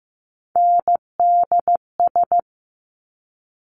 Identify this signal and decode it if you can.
Morse code, heard as a keyed tone, about 15 wpm, 700 Hz. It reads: NDS
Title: Top Three Letter Combinations in Words 15wpm